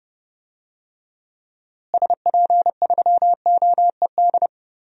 Morse code, heard as a keyed tone, about 30 wpm, 700 Hz. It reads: SP3OED
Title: DXCC-100 Call Signs - 6 Characters - Repeated 3x - 30wpm